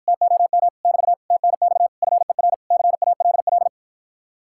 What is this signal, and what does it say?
Morse code, heard as a keyed tone, about 54 wpm, 700 Hz. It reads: TOM <BT> TNX FER CALL